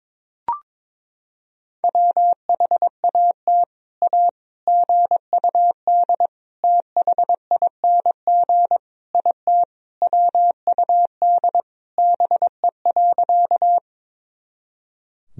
Morse code, heard as a keyed tone, about 22 wpm, 700 Hz. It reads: WHAT A GUD THING IT WUD BE.
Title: Sentences from Top 200 Words and Abbreviations 22wpm